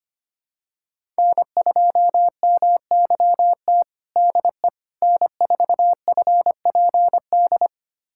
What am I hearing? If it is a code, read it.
Morse code, heard as a keyed tone, about 25 wpm, 700 Hz. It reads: N2MYT DE N4FPD
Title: USA Callsign DE Callsign - 5 Characters - 25wpm